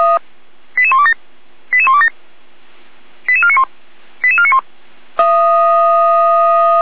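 Five-tone selective call (ZVEI), heard as quick runs of five tones, tones of about 70 ms. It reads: 88127 88127 88491 88491
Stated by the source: ZVEI five-tone call 88199 received by SDR, fire brigade Feuerwehr Orlamünde.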